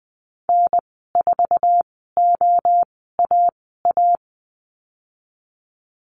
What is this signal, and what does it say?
Morse code, heard as a keyed tone, about 20 wpm, spaced as for 14 wpm, 700 Hz. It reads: N4OAA
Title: 173 Lesson 28g - Number 9 - Call Signs 20wpm